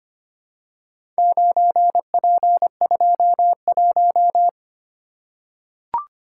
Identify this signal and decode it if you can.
Morse code, heard as a keyed tone, about 25 wpm, 700 Hz. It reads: 9P21
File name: practice / 25wpm / Four Letter-Number Combinations - 25wpm